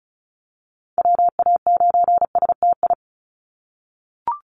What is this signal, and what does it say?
Morse code, heard as a keyed tone, about 34 wpm, 700 Hz. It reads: WA9STI
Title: NA SOTA Activator Callsigns 35wpm